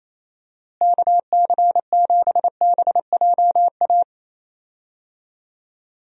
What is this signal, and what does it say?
Morse code, heard as a keyed tone, about 28 wpm, 700 Hz. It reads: KC7BJA
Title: USA Call Signs - 6 Characters - Repeated 3x - 28wpm